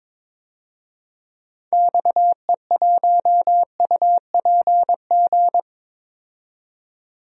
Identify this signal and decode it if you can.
Morse code, heard as a keyed tone, about 21 wpm, 700 Hz. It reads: XE1UPG
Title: DXCC-100 Call Signs - 6 Characters - 22wpm